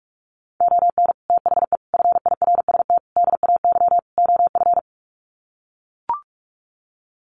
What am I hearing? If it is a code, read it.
Morse code, heard as a keyed tone, about 44 wpm, 700 Hz. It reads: ON THE FIRST DAY OF
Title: Sets of 5 Words Encore 45wpm